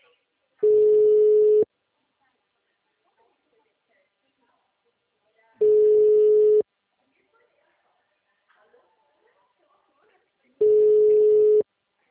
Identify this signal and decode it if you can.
Ringback tone (Europe, 425 Hz), with speech.
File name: out- -2013- - - .481